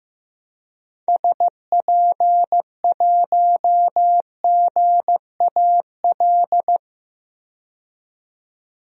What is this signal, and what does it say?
Morse code, heard as a keyed tone, about 15 wpm, 700 Hz. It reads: SP1GAL
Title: DXCC-100 Call Signs - 6 Characters - 15wpm